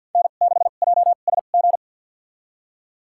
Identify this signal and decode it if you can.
Morse code, heard as a keyed tone, about 52 wpm, spaced as for 36 wpm, 700 Hz. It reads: N6JSG
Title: USA Call Signs - 5 Characters - 2x Character Spacing - 50wpm